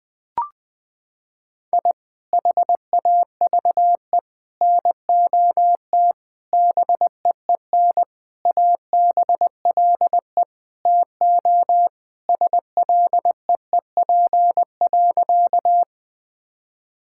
Morse code, heard as a keyed tone, about 20 wpm, 700 Hz. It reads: I HAVE NOT BEEN ABLE TO SLEEP.
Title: Sentences with 5-Word Phrases - 20wpm